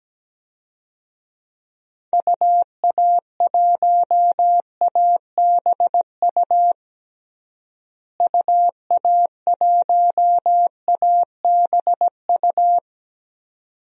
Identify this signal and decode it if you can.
Morse code, heard as a keyed tone, about 17 wpm, 700 Hz. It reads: UA1ABU UA1ABU
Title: DXCC-100 Call Signs - 6 Characters - Repeated 2x - 17wpm